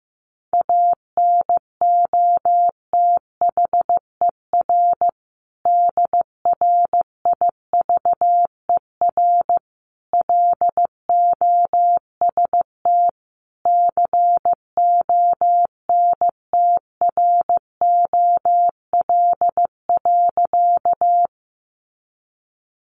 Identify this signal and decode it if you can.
Morse code, heard as a keyed tone, about 15 wpm, 700 Hz. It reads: ANOTHER DRIVER LOST CONTROL.
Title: Sentences from Top 1400 Words 15wpm